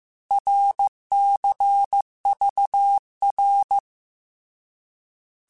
Morse code, heard as a keyed tone, about 15 wpm, 800 Hz. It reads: RCVR
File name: Abreviations / Abreviations mp3s / Abreviations@13